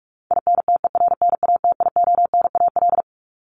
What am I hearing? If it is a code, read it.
Morse code, heard as a keyed tone, about 44 wpm, 700 Hz. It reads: INTERNATIONAL
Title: Top 501-750 Words 45wpm